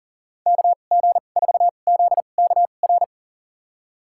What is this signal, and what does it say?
Morse code, heard as a keyed tone, about 40 wpm, spaced as for 29 wpm, 700 Hz. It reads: KG4ZKR
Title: USA Call Signs - 6 Characters - 2x Character Spacing - 40wpm